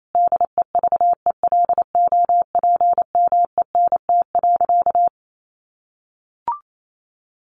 Morse code, heard as a keyed tone, about 28 wpm, 700 Hz. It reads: DEVELOPMENT.
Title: Sentences from Top 1400 Words 28wpm